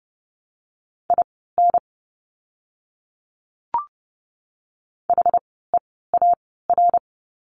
Morse code, heard as a keyed tone, about 30 wpm, spaced as for 17 wpm, 700 Hz. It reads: IN HEAR
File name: course / 30:10wpm / 115 Lesson 19f - Letter w - Words 30wpm